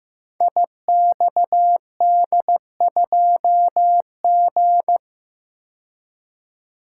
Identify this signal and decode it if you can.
Morse code, heard as a keyed tone, about 15 wpm, 700 Hz. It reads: IXD2G
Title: Code Groups - 5 Letter:Numbers - 15wpm